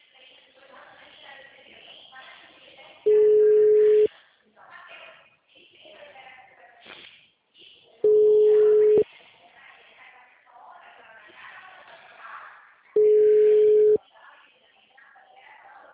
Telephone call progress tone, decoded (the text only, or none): ringback tone